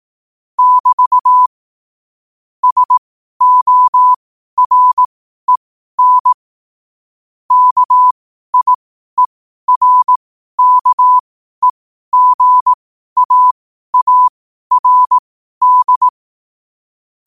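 Morse code, heard as a keyed tone, about 18 wpm, spaced as for 12 wpm, 1000 Hz. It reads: <BT> SOREN KIERKEGAARD
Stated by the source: Morse code quote of the day at 12 wpm.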